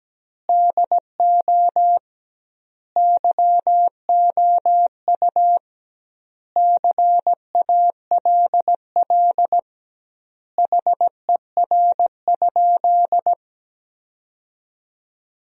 Morse code, heard as a keyed tone, about 17 wpm, 700 Hz. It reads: DO YOU CALL HER?
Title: 2x Word-Spacing - Sentences from Top 100 Words - 17wpm